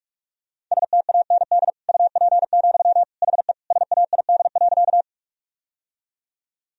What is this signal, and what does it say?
Morse code, heard as a keyed tone, about 41 wpm, 700 Hz. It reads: STAND UP, HE SAID.
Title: Sentences from Top 1000 Words - Review Part 1 - 45wpm